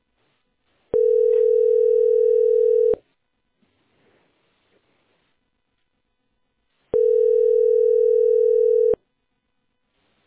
Ringback tone (North America), 440 + 480 Hz.